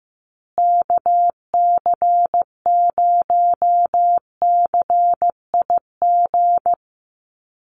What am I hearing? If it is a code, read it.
Morse code, heard as a keyed tone, about 15 wpm, 700 Hz. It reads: KC0CIG
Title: USA Call Signs Repeated 2x 15wpm